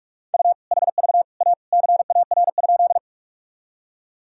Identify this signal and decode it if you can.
Morse code, heard as a keyed tone, about 43 wpm, 700 Hz. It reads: U HV A CAR?
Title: Sentences from Top 300 Words and Abbreviations 45wpm